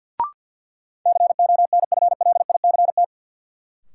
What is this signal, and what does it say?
Morse code, heard as a keyed tone, about 48 wpm, 700 Hz. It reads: CONFLICT
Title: Sentences from Top 1300 Words - New Word Drill Alt - 50wpm